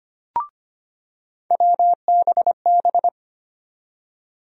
Morse code, heard as a keyed tone, about 25 wpm, 700 Hz. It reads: WBB
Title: USA Call Sign Postfixes - 3 Letters 25wpm